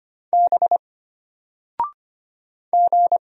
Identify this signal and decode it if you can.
Morse code, heard as a keyed tone, about 25 wpm, 700 Hz. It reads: B G
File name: Single-Letter-Number-25wpm